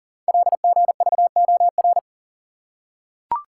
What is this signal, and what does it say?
Morse code, heard as a keyed tone, about 42 wpm, 690 Hz. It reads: LGVOR
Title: Code Groups - 5 Letters - 40wpm